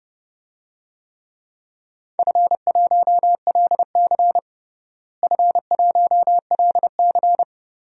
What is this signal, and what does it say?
Morse code, heard as a keyed tone, about 29 wpm, 700 Hz. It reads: F1LC F1LC
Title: DXCC-100 Call Signs - 4 Characters - Repeated 3x - 30wpm